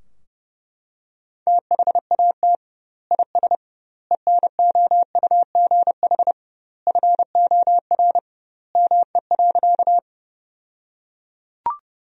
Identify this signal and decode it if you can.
Morse code, heard as a keyed tone, about 32 wpm, 700 Hz. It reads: THAT IS ENOUGH FOR ME.
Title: 2x Word-Spacing - Sentences from Top 400 Words - 30wpm